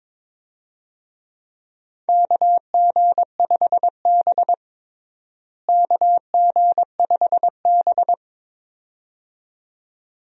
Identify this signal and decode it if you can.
Morse code, heard as a keyed tone, about 22 wpm, 700 Hz. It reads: KG5B KG5B
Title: USA Call Signs - 4 Characters - Repeated 2x - 22wpm